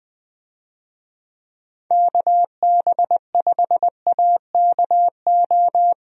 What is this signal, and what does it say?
Morse code, heard as a keyed tone, about 21 wpm, 700 Hz. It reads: KB5AKO
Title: USA Call Signs - 6 Characters - Repeated 2x - 20wpm